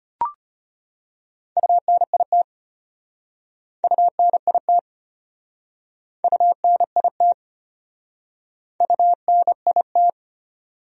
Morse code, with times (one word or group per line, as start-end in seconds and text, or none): 1.6-2.4 UNIT
3.8-4.8 UNIT
6.2-7.3 UNIT
8.8-10.1 UNIT